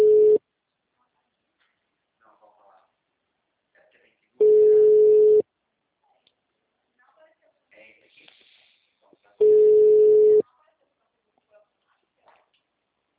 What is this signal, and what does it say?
Ringback tone (Europe, 425 Hz), with speech.